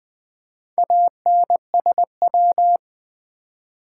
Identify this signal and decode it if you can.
Morse code, heard as a keyed tone, about 20 wpm, 700 Hz. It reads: ANSW
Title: Build A Word - 2 Letters At A Time - 6-Letter Words - 20wpm